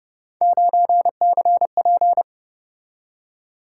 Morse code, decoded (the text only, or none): 9CP